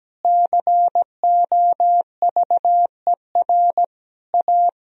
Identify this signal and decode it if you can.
Morse code, heard as a keyed tone, about 17 wpm, 700 Hz. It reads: COVER A
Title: Sentences from Top 1000 Words - Review Part 2 - 17wpm